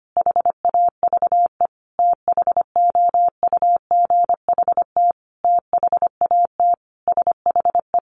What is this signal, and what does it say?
Morse code, heard as a keyed tone, about 25 wpm, 700 Hz. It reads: HAVE THOUGHT THAT SHE
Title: Sentences from Top 1200 Words 25wpm